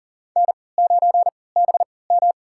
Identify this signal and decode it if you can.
Morse code, heard as a keyed tone, about 40 wpm, spaced as for 23 wpm, 700 Hz. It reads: N9BM
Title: USA Call Signs - 4 Characters - 3x Character Spacing - 40wpm